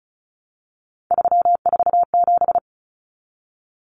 Morse code, heard as a keyed tone, about 34 wpm, 700 Hz. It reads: 347
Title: Numbers - 3-Digit 35wpm